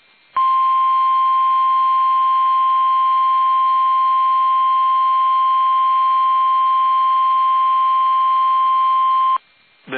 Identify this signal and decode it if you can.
Emergency Alert System (SAME) tones (the 1050 Hz attention tone). It is heard from 0.3 to 9.4 s.
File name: eas / KUSO / KUSO